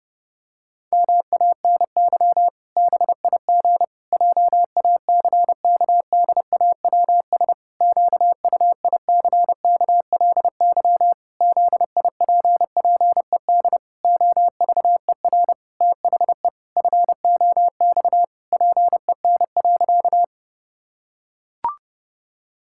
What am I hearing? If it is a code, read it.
Morse code, heard as a keyed tone, about 32 wpm, 700 Hz. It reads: MANY BIG JACKDAWS QUICKLY ZIPPED OVER THE FOX PEN.
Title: Pangram Challenge v2 - 30wpm